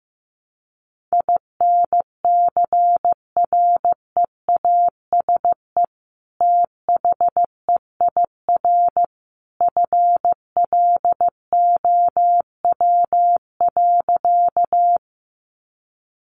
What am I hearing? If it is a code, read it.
Morse code, heard as a keyed tone, about 15 wpm, 700 Hz. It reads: INCREASE THEIR FLOW.
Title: Sentences from Top 1200 Words 15wpm